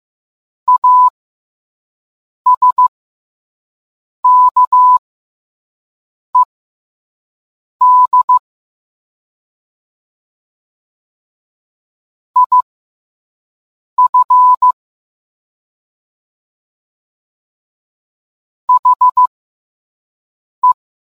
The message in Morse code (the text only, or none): ASKED IF HE